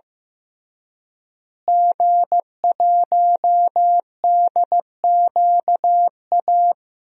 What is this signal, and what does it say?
Morse code, heard as a keyed tone, about 15 wpm, 700 Hz. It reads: G1DQA
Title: DXCC-100 Call Signs - 5 Characters - Repeated 2x - 15wpm